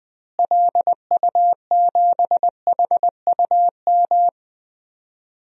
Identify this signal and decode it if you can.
Morse code, heard as a keyed tone, about 20 wpm, 700 Hz. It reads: LU7HUM
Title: DXCC-100 Call Signs - 6 Characters - Repeated 2x - 20wpm